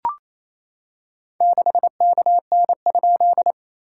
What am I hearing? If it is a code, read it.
Morse code, heard as a keyed tone, about 28 wpm, 700 Hz. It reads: 6KN?
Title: SOTA QSO Practice 28wpm